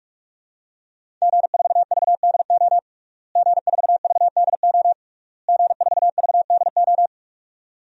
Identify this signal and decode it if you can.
Morse code, heard as a keyed tone, about 46 wpm, 700 Hz. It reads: G4VDO G4VDO G4VDO
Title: DXCC-100 Call Signs - 5 Characters - Repeated 3x - 45wpm